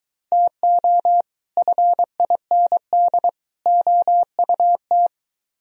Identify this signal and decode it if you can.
Morse code, heard as a keyed tone, about 23 wpm, 700 Hz. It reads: TO FIND OUT